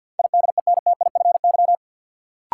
Morse code, heard as a keyed tone, about 52 wpm, 700 Hz. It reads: IDENTIFY
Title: Sentences from Top 900 Words - New Word Drill - 50wpm